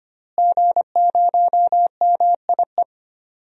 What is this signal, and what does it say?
Morse code, heard as a keyed tone, about 25 wpm, 700 Hz. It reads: G0MIE
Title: DXCC-100 Call Signs - 5 Characters - Repeated 3x - 25wpm